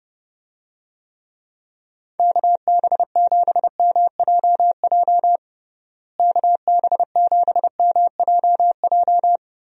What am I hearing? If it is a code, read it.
Morse code, heard as a keyed tone, about 31 wpm, 700 Hz. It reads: KB7MJJ KB7MJJ
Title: USA Call Signs - 6 Characters - Repeated 3x - 30wpm